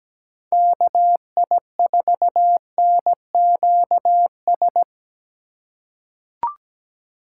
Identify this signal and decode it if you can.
Morse code, heard as a keyed tone, about 17 wpm, 700 Hz. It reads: KI4NQS
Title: USA Call Signs - 6 Characters - 2x Character Spacing - 17wpm